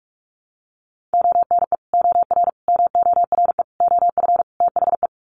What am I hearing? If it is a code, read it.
Morse code, heard as a keyed tone, about 44 wpm, 700 Hz. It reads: ONE OR MORE OF THE
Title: Sets of 5 Words Encore 45wpm